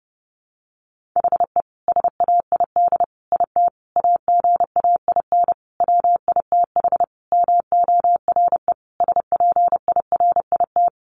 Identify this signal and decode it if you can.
Morse code, heard as a keyed tone, about 30 wpm, 700 Hz. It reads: HE SAID IT AGAIN WITH MORE SPIRIT
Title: Sentences from Top 1400 Words 30wpm